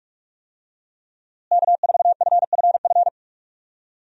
Morse code, heard as a keyed tone, about 46 wpm, 700 Hz. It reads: K4FFF
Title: USA Call Signs - 5 Characters - Repeated 3x - 45wpm